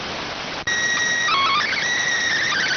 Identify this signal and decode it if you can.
SSTV (slow-scan television), PD 290.